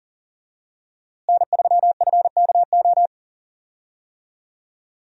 Morse code, heard as a keyed tone, about 40 wpm, 700 Hz. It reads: N3FKO
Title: USA Call Signs - 5 Characters - Repeated 2x - 40wpm